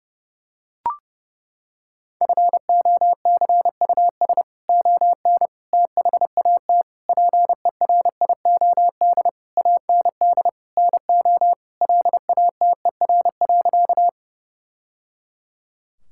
Morse code, heard as a keyed tone, about 30 wpm, 700 Hz. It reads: FOCUS ON THAT PERIOD AND NO LATER.
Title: Sentences from Top 1000 Words - Review Part 5 - 30wpm